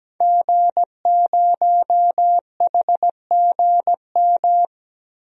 Morse code, heard as a keyed tone, about 17 wpm, 700 Hz. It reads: G0HGM
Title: DXCC-100 Call Signs - 5 Characters - 17wpm